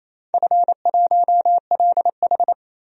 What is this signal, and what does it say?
Morse code, heard as a keyed tone, about 28 wpm, 700 Hz. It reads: F1LH
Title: DXCC-100 Call Signs - 4 Characters - 28wpm